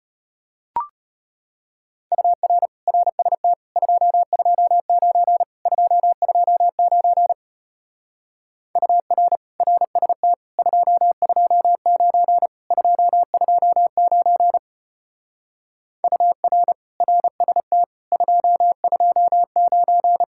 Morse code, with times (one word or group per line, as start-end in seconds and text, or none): 2.1-2.7 UR
2.9-3.5 RST
3.8-5.4 229
5.7-7.3 229
8.8-9.4 UR
9.6-10.3 RST
10.6-12.5 229
12.7-14.6 229
16.0-16.7 UR
17.0-17.8 RST
18.1-20.2 229